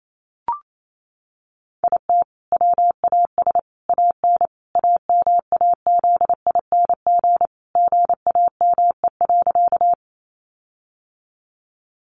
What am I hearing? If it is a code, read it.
Morse code, heard as a keyed tone, about 28 wpm, 700 Hz. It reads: IT WAS AN AMAZING GAME.
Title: Sentences from Top 1500 Words 28wpm